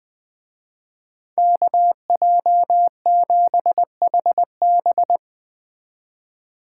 Morse code, heard as a keyed tone, about 20 wpm, 700 Hz. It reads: KJ7HB
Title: Speed Racing - USA 2x2 Call Signs 17wpm